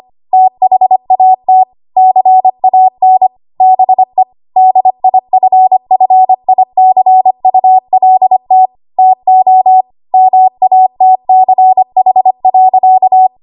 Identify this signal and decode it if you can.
Morse code, heard as a keyed tone, about 25 wpm, 750 Hz. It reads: THAT CAN BE DIFFICULT TO MATCH.
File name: Archive / 25 WPM / 25WPM